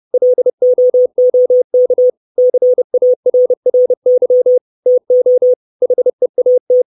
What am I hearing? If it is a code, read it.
Morse code, heard as a keyed tone, about 30 wpm, 500 Hz. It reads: LOOK CARRY TO HEAT
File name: 632 Words 30WPM